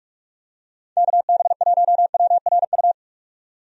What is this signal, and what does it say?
Morse code, heard as a keyed tone, about 46 wpm, 700 Hz. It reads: KB1WRU
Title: USA Call Signs - 6 Characters - 45wpm